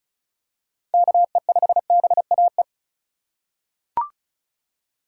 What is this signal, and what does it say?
Morse code, heard as a keyed tone, about 34 wpm, 700 Hz. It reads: KE5BAE
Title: USA Call Signs - 6 Characters - 8x Character Spacing - 35wpm